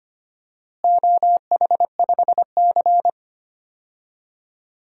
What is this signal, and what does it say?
Morse code, heard as a keyed tone, about 25 wpm, 700 Hz. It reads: OH5C
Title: DXCC-100 Call Signs - 4 Characters - Repeated 3x - 25wpm